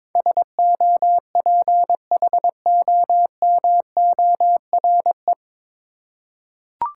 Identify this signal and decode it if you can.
Morse code, heard as a keyed tone, about 22 wpm, 700 Hz. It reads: SOPHOMORE
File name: 9 Letter Words - First 3 Letters Given - 22wpm